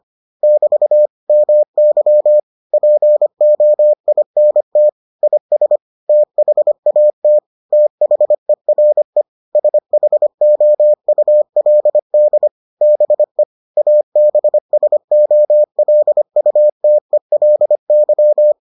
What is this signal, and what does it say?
Morse code, heard as a keyed tone, about 25 wpm, 600 Hz. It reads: <BT> MY POINT IS THAT THERE SHOULD BE ABSOLUTELY